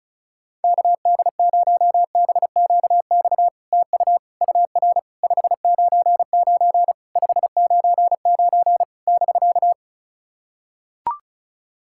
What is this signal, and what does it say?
Morse code, heard as a keyed tone, about 38 wpm, 700 Hz. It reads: KD0BQX TU UR 599 599 <BK>
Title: SOTA QSO Practice 35wpm